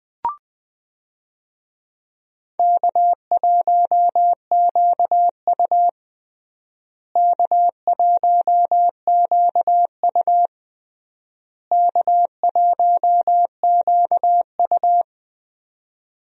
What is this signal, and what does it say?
Morse code, heard as a keyed tone, about 20 wpm, 700 Hz. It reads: K1QU K1QU K1QU
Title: USA Call Signs - 4 Characters - Repeated 3x - 20wpm